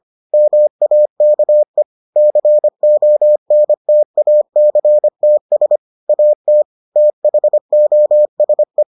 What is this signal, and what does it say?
Morse code, heard as a keyed tone, about 25 wpm, 600 Hz. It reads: MAKE CONTACTS AT THOSE